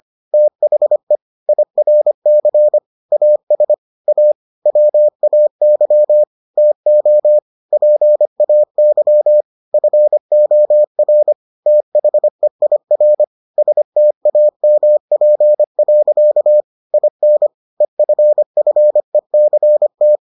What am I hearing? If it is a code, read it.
Morse code, heard as a keyed tone, about 25 wpm, 600 Hz. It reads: THE IRC AS A WAY TO PAY FOR THEIR STAMP. IN EFFECT